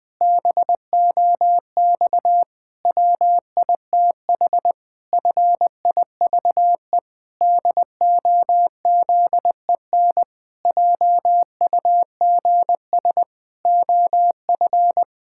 Morse code, heard as a keyed tone, about 20 wpm, 700 Hz. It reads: BOX WITH FIVE DOZEN JUGS OF
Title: Pangram Challenge v2 - 20wpm